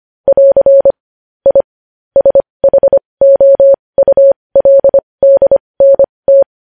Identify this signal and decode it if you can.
Morse code, heard as a keyed tone, about 25 wpm, spaced as for 20 wpm, 550 Hz. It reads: <AR> I SHOULDNT
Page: News Headlines in Morse Code at 20 WPM